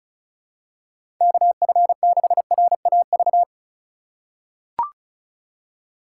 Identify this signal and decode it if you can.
Morse code, heard as a keyed tone, about 36 wpm, 690 Hz. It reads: KF6RAV
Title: USA Call Signs - 6 Characters - 3x Character Spacing - 35wpm